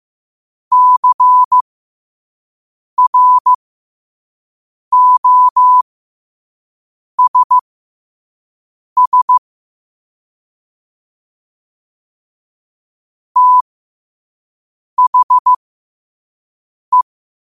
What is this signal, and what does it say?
Morse code, heard as a keyed tone, about 15 wpm, spaced as for 5 wpm, 1000 Hz. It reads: CROSS THE
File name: qotdfiles / 5wpm